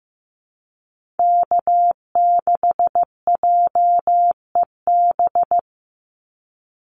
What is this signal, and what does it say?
Morse code, heard as a keyed tone, about 15 wpm, 700 Hz. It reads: K6JEB
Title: Call Signs Repeated 3x - 15wpm